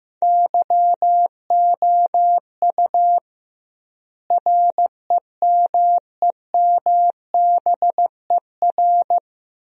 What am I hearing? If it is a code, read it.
Morse code, heard as a keyed tone, about 15 wpm, 700 Hz. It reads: YOU REMEMBER
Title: 2x Word-Spacing - Sentences from Top 400 Words - 15wpm